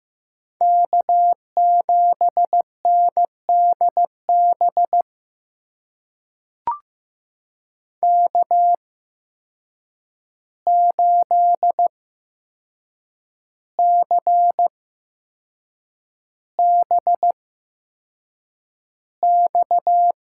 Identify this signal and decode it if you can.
Morse code, heard as a keyed tone, about 15 wpm, spaced as for 6 wpm, 700 Hz. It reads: K7NDB K8CBX
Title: USA Call Signs - 8x Character Spacing - 15wpm